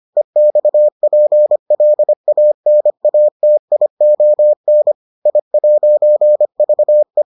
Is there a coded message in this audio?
Yes: Morse code from 0.2 to 7.2 s, heard as a keyed tone, about 25 wpm, 600 Hz. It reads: EXPLANATION I'VE